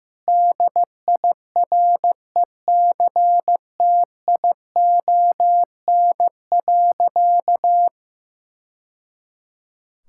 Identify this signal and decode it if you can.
Morse code, heard as a keyed tone, about 15 wpm, 700 Hz. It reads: DIRECTION.
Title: Sentences from Top 1000 Words - Review Part 6 - 15wpm